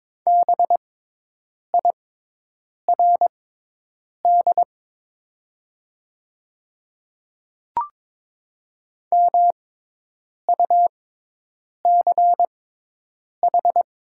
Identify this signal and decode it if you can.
Morse code, heard as a keyed tone, about 22 wpm, spaced as for 8 wpm, 700 Hz. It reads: BIRD MUCH
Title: 6x Character Spacing - Top 500 Words - 22wpm